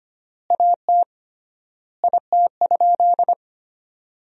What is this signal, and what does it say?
Morse code, heard as a keyed tone, about 25 wpm, 700 Hz. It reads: AT IT?
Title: 3x Word-Spacing - Sentences from Top 100 Words - 25wpm